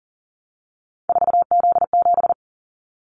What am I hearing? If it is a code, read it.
Morse code, heard as a keyed tone, about 40 wpm, 700 Hz. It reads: 4Z7
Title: Callsign Prefixes for Top 50 DXCC - 40wpm